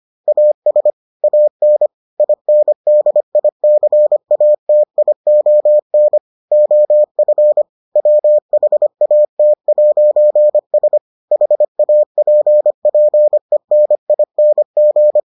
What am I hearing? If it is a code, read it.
Morse code, heard as a keyed tone, about 25 wpm, 600 Hz. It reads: AS AN INDICATION OF WHAT'S HAPPENING